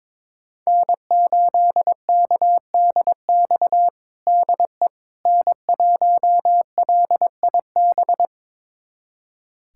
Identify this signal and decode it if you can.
Morse code, heard as a keyed tone, about 22 wpm, 700 Hz. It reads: N8KDX DE N1LIB
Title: USA Callsign DE Callsign - 5 Characters - 22wpm